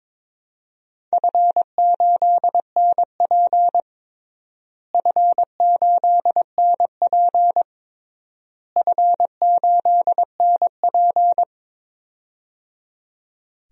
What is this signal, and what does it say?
Morse code, heard as a keyed tone, about 22 wpm, 700 Hz. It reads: F8NP F8NP F8NP
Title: DXCC-100 Call Signs - 4 Characters - Repeated 3x - 22wpm